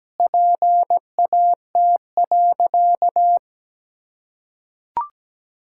Morse code, heard as a keyed tone, about 17 wpm, 700 Hz. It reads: PAT.
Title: Pangram Challenge v2 - 17wpm